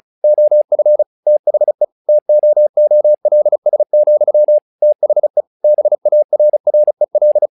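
Morse code, heard as a keyed tone, about 36 wpm, 600 Hz. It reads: OF THE TOOLS, THE BARREL
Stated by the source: Morse practice text "Tower Trouble" sent at 35 WPM, from the SKCC - Straight Key Century Club learning center audio files.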